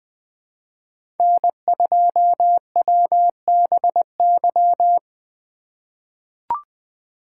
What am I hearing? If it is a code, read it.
Morse code, heard as a keyed tone, about 21 wpm, 700 Hz. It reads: N2WBY
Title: USA Call Signs - 4x Character Spacing - 20wpm